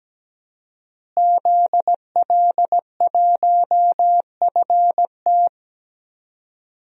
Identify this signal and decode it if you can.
Morse code, heard as a keyed tone, about 17 wpm, 700 Hz. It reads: ZL1FT
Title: DXCC-100 Call Signs - 5 Characters - Repeated 2x - 17wpm